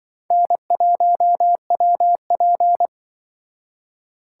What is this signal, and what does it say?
Morse code, heard as a keyed tone, about 25 wpm, 700 Hz. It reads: N1WP
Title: Speed Racing - USA 1x2 Call Signs 20wpm